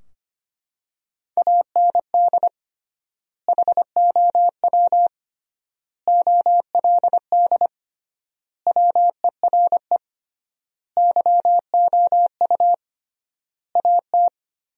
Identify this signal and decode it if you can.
Morse code, heard as a keyed tone, about 25 wpm, 700 Hz. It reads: AND HOW OLD WERE YOU AT
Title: 3x Word-Spacing - Sentences from Top 200 Words - 25wpm